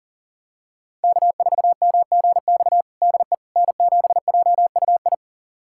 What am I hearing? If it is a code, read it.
Morse code, heard as a keyed tone, about 40 wpm, 700 Hz. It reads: K4MGX DE N7JUI
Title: USA Callsign DE Callsign - 5 Characters - 40wpm